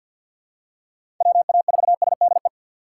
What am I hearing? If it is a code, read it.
Morse code, heard as a keyed tone, about 51 wpm, 690 Hz. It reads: WA4SDE
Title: USA Call Signs - 6 Characters - 3x Character Spacing - 50wpm